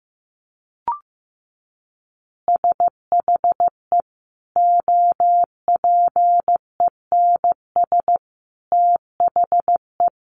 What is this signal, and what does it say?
Morse code, heard as a keyed tone, about 15 wpm, 700 Hz. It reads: SHE OPENS THE
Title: Easy Phrases 15wpm